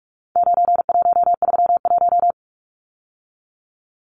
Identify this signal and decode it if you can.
Morse code, heard as a keyed tone, about 44 wpm, 700 Hz. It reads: 9131
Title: Numbers - 4-Digit 45wpm